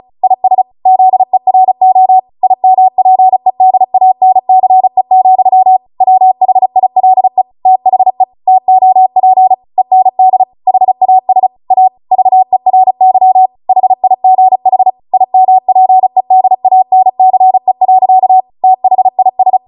Morse code, heard as a keyed tone, about 36 wpm, 750 Hz. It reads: IS ZERO IMPEDANCE, WHILE THE TOP END HAS A VERY HIGH IMPEDANCE. THIS